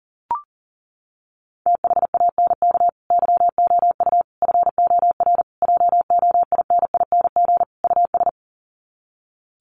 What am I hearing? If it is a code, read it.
Morse code, heard as a keyed tone, about 40 wpm, 700 Hz. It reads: THANK YOU FOR JOINING US
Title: Sets of 5 Words 40wpm